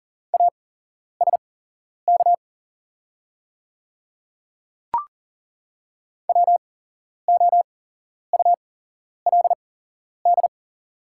Morse code, heard as a keyed tone, about 37 wpm, spaced as for 11 wpm, 690 Hz. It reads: ASK WOULD
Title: 8x Character Spacing - Top 300 Words - 40wpm